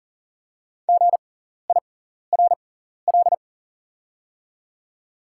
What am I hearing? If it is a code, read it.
Morse code, heard as a keyed tone, about 40 wpm, spaced as for 14 wpm, 690 Hz. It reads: GIRL
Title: 6x Character Spacing - Top 500 Words - 40wpm